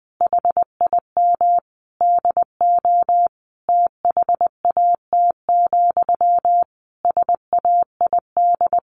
Morse code, heard as a keyed tone, about 20 wpm, 700 Hz. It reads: HIM DO THAT, SAID
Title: A Sound Of Thunder 20wpm